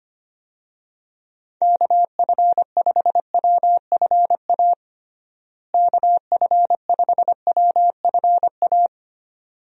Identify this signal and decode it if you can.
Morse code, heard as a keyed tone, about 25 wpm, 700 Hz. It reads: KF5WFA KF5WFA
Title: USA Call Signs - 6 Characters - Repeated 2x - 25wpm